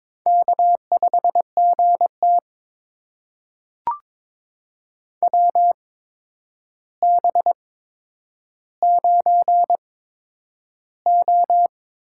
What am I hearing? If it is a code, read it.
Morse code, heard as a keyed tone, about 22 wpm, spaced as for 9 wpm, 700 Hz. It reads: K5GT WB9O
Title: USA Call Signs - 4 Characters - 8x Character Spacing - 22wpm